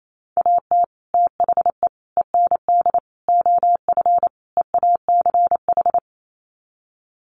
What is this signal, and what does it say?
Morse code, heard as a keyed tone, about 28 wpm, 700 Hz. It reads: AT THE END OF EACH
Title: Sets of 5 Words Encore 28wpm